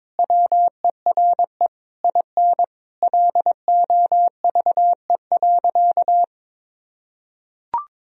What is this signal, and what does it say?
Morse code, heard as a keyed tone, about 21 wpm, 700 Hz. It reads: WERE IN LOVE.
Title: Sentences from Top 1000 Words - Review Part 4 - 22wpm